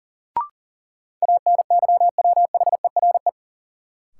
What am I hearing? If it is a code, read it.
Morse code, heard as a keyed tone, about 40 wpm, 700 Hz. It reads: ANYWHERE
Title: Sentences from Top 1400 Words - New Word Drill - 40wpm